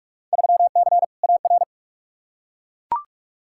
Morse code, heard as a keyed tone, about 41 wpm, 700 Hz. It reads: 3C AR
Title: Field Day Exchanges Encore - 45wpm